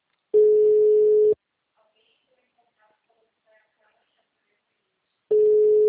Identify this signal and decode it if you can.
Ringback tone (Europe, 425 Hz), with speech.